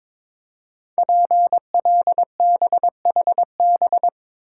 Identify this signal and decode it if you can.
Morse code, heard as a keyed tone, about 22 wpm, 700 Hz. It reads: PLBHB
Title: Code Groups - 5 Letter:Numbers - 22wpm